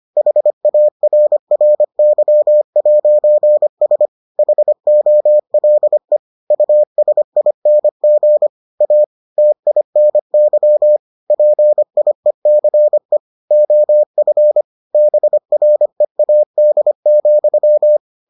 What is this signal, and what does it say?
Morse code, heard as a keyed tone, about 25 wpm, 600 Hz. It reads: HARRY'S HOLE USING A TINY PIECE OF BREAD,